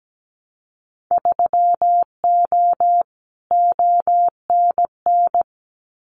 Morse code, heard as a keyed tone, about 17 wpm, 700 Hz. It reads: 3O ONN
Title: Winter Field Day Exchanges 17wpm